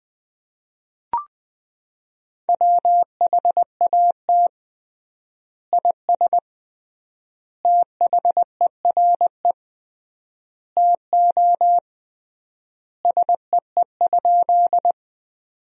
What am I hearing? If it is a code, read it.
Morse code, heard as a keyed tone, about 20 wpm, 700 Hz. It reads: WHAT IS THERE TO SEE?
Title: 3x Word-Spacing - Sentences from Top 100 Words - 20wpm